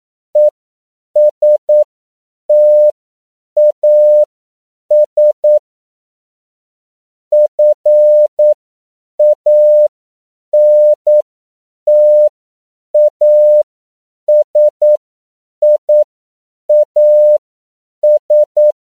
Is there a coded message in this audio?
Yes: Morse code from 0.4 to 18.7 s, heard as a keyed tone, about 9 wpm, spaced as for 7 wpm, 600 Hz. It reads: ESTAS FANTASIAS